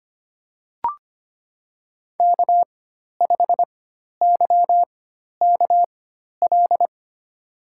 Morse code, heard as a keyed tone, about 25 wpm, spaced as for 12 wpm, 700 Hz. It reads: K5YKL
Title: USA Call Signs - 5 Characters - 4x Character Spacing - 25wpm